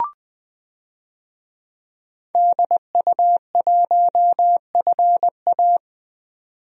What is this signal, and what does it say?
Morse code, heard as a keyed tone, about 21 wpm, 700 Hz. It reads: DU1FA
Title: DXCC-100 Call Signs - 5 Characters - 20wpm